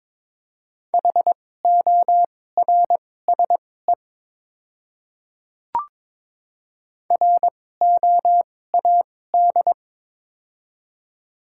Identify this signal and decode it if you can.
Morse code, heard as a keyed tone, about 21 wpm, spaced as for 16 wpm, 700 Hz. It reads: HORSE ROAD
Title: 2x Character Spacing - Top 500 Words - 22wpm